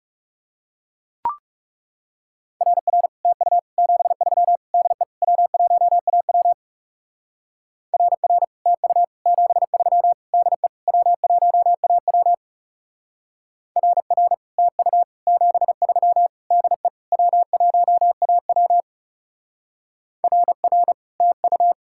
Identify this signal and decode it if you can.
Morse code, heard as a keyed tone, about 37 wpm, 700 Hz. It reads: RR TU 73 DE W1AW RR TU 73 DE W1AW RR TU 73 DE W1AW RR TU
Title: Speed-Racing Common QSO Phrases Encore - 30wpm